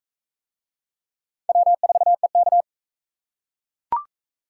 Morse code, heard as a keyed tone, about 37 wpm, 700 Hz. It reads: W4EK